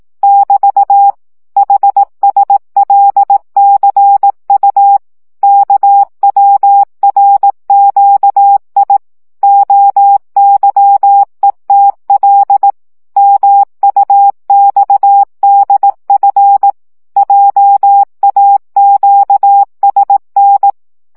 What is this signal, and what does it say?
Morse code, heard as a keyed tone, about 18 wpm, 800 Hz. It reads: <BT> HSLCU KWRQI OYETL MUXDF JAQSN
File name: morze vizsga 2011-3 15